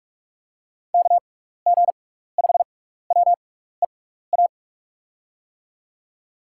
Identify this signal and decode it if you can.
Morse code, heard as a keyed tone, about 44 wpm, spaced as for 15 wpm, 700 Hz. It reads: KG5WEA
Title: USA Call Signs - 6 Characters - 6x Character Spacing - 45wpm